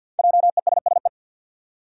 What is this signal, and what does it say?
Morse code, heard as a keyed tone, about 49 wpm, 700 Hz. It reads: JESSE
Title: USA Common Names - Max 5 Letters - 50wpm